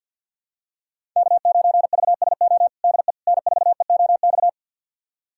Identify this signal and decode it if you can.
Morse code, heard as a keyed tone, about 52 wpm, 700 Hz. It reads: K9VSO DE N4EOX
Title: USA Callsign DE Callsign - 5 Characters - 50wpm